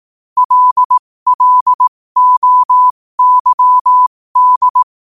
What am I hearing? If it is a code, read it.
Morse code, heard as a keyed tone, about 18 wpm, spaced as for 15 wpm, 1000 Hz. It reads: LLOYD
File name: qotdfiles / 15wpm